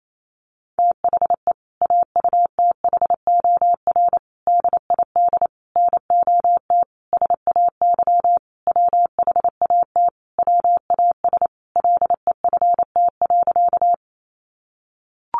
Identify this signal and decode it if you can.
Morse code, heard as a keyed tone, about 28 wpm, 700 Hz. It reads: THE AUTHOR DID NOT SAY WHAT WAS LEFT.
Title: Sentences from Top 1200 Words 28wpm